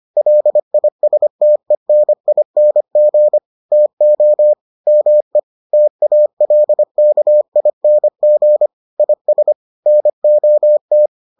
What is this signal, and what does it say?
Morse code, heard as a keyed tone, about 25 wpm, 600 Hz. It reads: LISTENING TO ME TALKING IS NOT